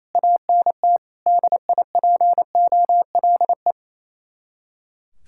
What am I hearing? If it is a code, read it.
Morse code, heard as a keyed tone, about 35 wpm, 700 Hz. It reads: ANT DIPOLE